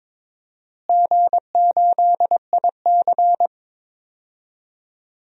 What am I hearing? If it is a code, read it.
Morse code, heard as a keyed tone, about 22 wpm, 700 Hz. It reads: G8IC
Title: DXCC-100 Call Signs - 4 Characters - Repeated 2x - 22wpm